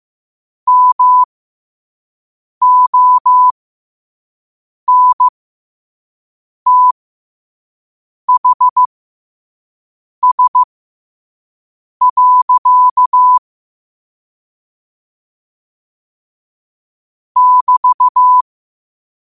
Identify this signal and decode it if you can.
Morse code, heard as a keyed tone, about 15 wpm, spaced as for 5 wpm, 1000 Hz. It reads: MONTHS. <BT>
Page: New quotes every day in morse code at 5 Words per minute.